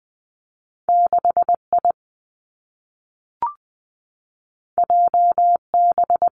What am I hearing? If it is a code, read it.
Morse code, heard as a keyed tone, about 20 wpm, 700 Hz. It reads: 6 I J B
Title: Two Challenging Letter-Number Combinations 20wpm